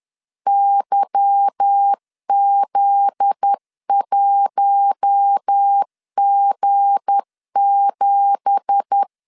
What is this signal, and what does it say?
Morse code, heard as a keyed tone, about 10 wpm, 800 Hz. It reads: YZ1G7
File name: CW / DET / 1 9 1500C S A 10wpm 4511